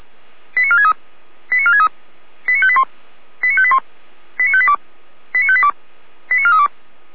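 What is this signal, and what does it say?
Five-tone selective call (ZVEI), heard as quick runs of five tones, tones of about 70 ms. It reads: 89463 89463 89581 89581 89582 89582 89432
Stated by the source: Five-tone ZVEI selective call, 455 channel, received by SDR, from feuerwehr-orlamuende.org fire brigade audio archive.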